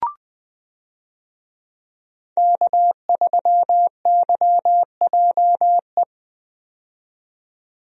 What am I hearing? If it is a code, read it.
Morse code, heard as a keyed tone, about 20 wpm, 700 Hz. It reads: K3YJE
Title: USA Call Signs - 5 Characters - 20wpm